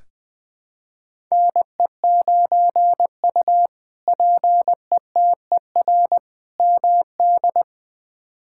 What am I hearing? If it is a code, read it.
Morse code, heard as a keyed tone, about 20 wpm, 700 Hz. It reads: NE9U PETER MD
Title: K1USN SST QSO Practice - 20wpm